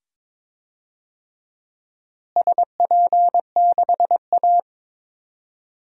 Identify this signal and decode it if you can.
Morse code, heard as a keyed tone, about 22 wpm, 700 Hz. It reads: SP6A
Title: DXCC-100 Call Signs - 4 Characters - Repeated 2x - 22wpm